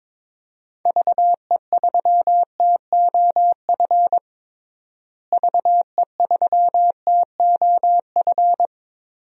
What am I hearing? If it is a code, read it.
Morse code, heard as a keyed tone, about 22 wpm, 700 Hz. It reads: VE3TOF VE3TOF
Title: DXCC-100 Call Signs - 6 Characters - Repeated 3x - 22wpm